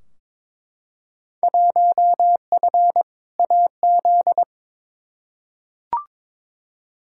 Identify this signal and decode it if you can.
Morse code, heard as a keyed tone, about 22 wpm, 700 Hz. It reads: 1F AZ
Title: Field Day Exchanges - 22wpm